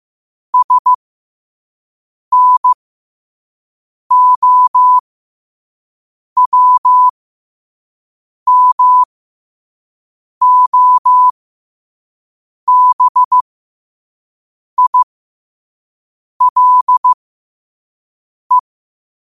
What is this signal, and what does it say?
Morse code, heard as a keyed tone, about 15 wpm, spaced as for 5 wpm, 1000 Hz. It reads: SNOWMOBILE
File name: qotdfiles / 5wpm